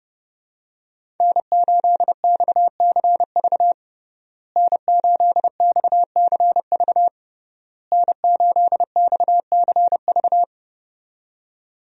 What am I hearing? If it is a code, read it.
Morse code, heard as a keyed tone, about 30 wpm, 700 Hz. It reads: N8XCV N8XCV N8XCV
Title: USA Call Signs - 5 Characters - Repeated 3x - 30wpm